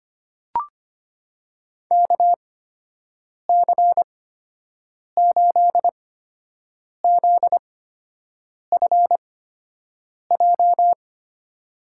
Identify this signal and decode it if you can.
Morse code, heard as a keyed tone, about 25 wpm, spaced as for 7 wpm, 700 Hz. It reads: KC8ZFJ
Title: USA Call Signs - 8x Character Spacing - 25wpm